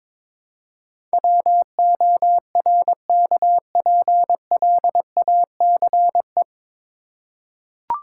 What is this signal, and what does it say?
Morse code, heard as a keyed tone, about 22 wpm, 700 Hz. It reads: WORKPLACE
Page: Morse Code Ninja practice 22wpm